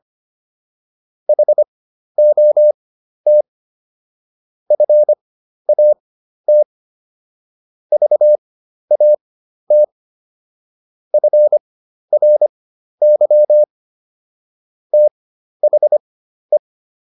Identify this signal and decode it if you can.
Morse code, heard as a keyed tone, about 25 wpm, spaced as for 12 wpm, 600 Hz. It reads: HOT FAT VAT FRY THE